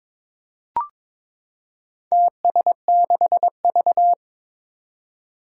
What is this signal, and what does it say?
Morse code, heard as a keyed tone, about 22 wpm, 700 Hz. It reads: TS6V
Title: Four Challenging Letter-Number Combinations 22wpm